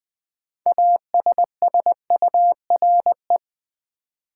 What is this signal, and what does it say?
Morse code, heard as a keyed tone, about 20 wpm, 690 Hz. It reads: ASSURE
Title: Build A Word - 3 Letters At A Time - 6-Letter Words - 20wpm